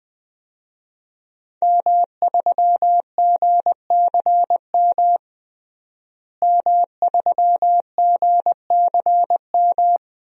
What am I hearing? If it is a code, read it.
Morse code, heard as a keyed tone, about 20 wpm, 700 Hz. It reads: M3GCM M3GCM
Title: DXCC-100 Call Signs - 5 Characters - Repeated 2x - 20wpm